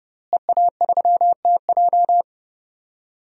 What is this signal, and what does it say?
Morse code, heard as a keyed tone, about 30 wpm, 700 Hz. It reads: EA3TJ
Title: DXCC-100 Call Signs - 5 Characters - Repeated 3x - 30wpm